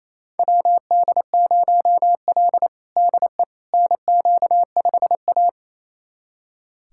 Morse code, heard as a keyed tone, about 28 wpm, 700 Hz. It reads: WD0L DE NQ5A
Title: USA Callsign DE Callsign - 4 Characters - 28wpm